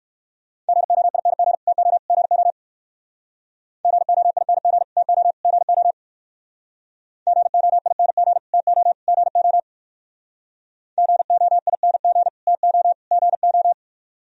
Morse code, heard as a keyed tone, about 58 wpm, 700 Hz. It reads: GOING TO GO GOING TO GO GOING TO GO GOING TO GO